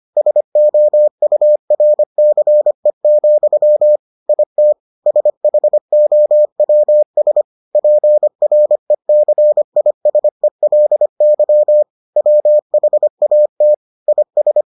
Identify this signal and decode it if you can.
Morse code, heard as a keyed tone, about 25 wpm, 600 Hz. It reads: SOURCE, IT SHOWS PRECISELY WHAT IS